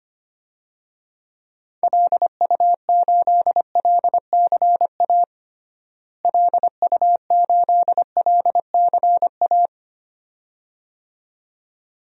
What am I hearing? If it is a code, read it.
Morse code, heard as a keyed tone, about 25 wpm, 700 Hz. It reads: LU8LCA LU8LCA
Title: DXCC-100 Call Signs - 6 Characters - Repeated 2x - 25wpm